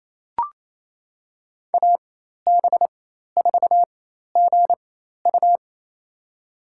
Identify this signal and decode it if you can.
Morse code, heard as a keyed tone, about 28 wpm, spaced as for 13 wpm, 700 Hz. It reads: AB4GU
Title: USA Call Signs - 4x Character Spacing - 28wpm